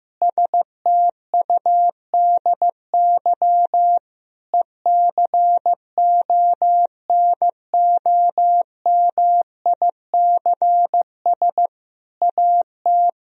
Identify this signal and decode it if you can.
Morse code, heard as a keyed tone, about 16 wpm, 700 Hz. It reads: STUDY ECONOMICS AT
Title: Sentences from Top 1000 Words - Review Part 3 - 15wpm